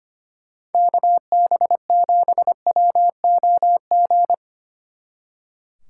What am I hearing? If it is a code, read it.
Morse code, heard as a keyed tone, about 31 wpm, 700 Hz. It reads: KB7WOG